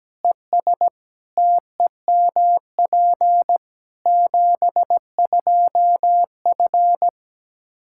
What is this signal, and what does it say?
Morse code, heard as a keyed tone, about 17 wpm, 700 Hz. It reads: ES TEMP 72F
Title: Practice QSO - Exchange 2 - Info - 17wpm